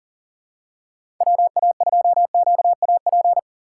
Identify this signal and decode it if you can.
Morse code, heard as a keyed tone, about 40 wpm, 700 Hz. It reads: WA2QAP
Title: USA Call Signs - 6 Characters - 2x Character Spacing - 40wpm